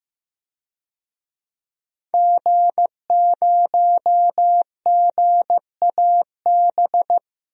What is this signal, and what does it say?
Morse code, heard as a keyed tone, about 15 wpm, 700 Hz. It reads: G0GAB
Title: DXCC-100 Call Signs - 5 Characters - 15wpm